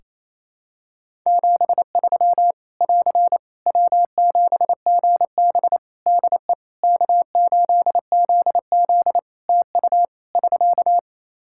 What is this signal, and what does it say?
Morse code, heard as a keyed tone, about 28 wpm, 700 Hz. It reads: 73 <AR> W7GB DE K8ZZ TU <SK>
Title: Practice QSO - Exchange 4 - Ending - 28wpm